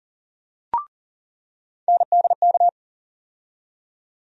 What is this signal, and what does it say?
Morse code, heard as a keyed tone, about 40 wpm, 700 Hz. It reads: NDK
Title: USA Call Sign Postfixes - 3 Letters 40wpm